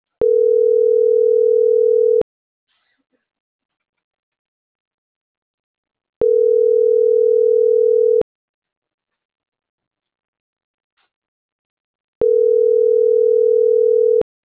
Ringback tone (North America), 440 + 480 Hz.